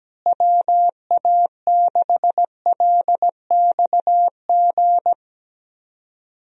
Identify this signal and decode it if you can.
Morse code, heard as a keyed tone, about 17 wpm, 700 Hz. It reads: WA6LXG
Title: USA Call Signs - 6 Characters - Repeated 3x - 17wpm